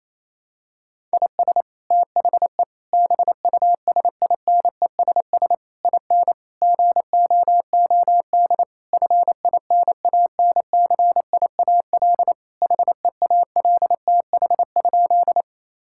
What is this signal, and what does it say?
Morse code, heard as a keyed tone, about 27 wpm, 700 Hz. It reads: IS THE BUSINESS IN GOOD FINANCIAL HEALTH?
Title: Sentences from Top 1000 Words - Review Part 5 - 28wpm